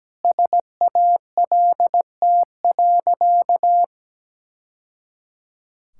Morse code, heard as a keyed tone, about 17 wpm, 700 Hz. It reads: SALT.
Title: Sentences from Top 1000 Words - Review Part 5 - 17wpm